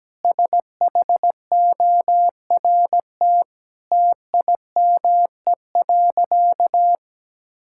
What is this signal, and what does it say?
Morse code, heard as a keyed tone, about 17 wpm, 700 Hz. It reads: SHORT TIME.
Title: Sentences from Top 1600 Words 17wpm